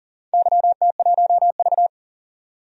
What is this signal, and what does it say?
Morse code, heard as a keyed tone, about 40 wpm, 700 Hz. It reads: YT1V
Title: DXCC-100 Call Signs - 4 Characters - 40wpm